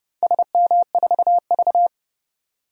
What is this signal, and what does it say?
Morse code, heard as a keyed tone, about 30 wpm, 700 Hz. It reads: SM4V
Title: DXCC-100 Call Signs - 4 Characters - 30wpm